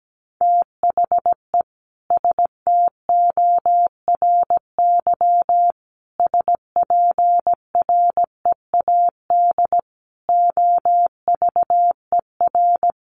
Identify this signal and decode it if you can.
Morse code, heard as a keyed tone, about 17 wpm, 700 Hz. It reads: THE STORY SPREAD OVER
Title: Sentences from Top 1200 Words 17wpm